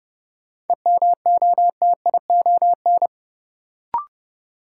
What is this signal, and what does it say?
Morse code, heard as a keyed tone, about 30 wpm, 700 Hz. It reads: EMOTION
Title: Sentences from Top 1600 Words - New Word Drill - 30wpm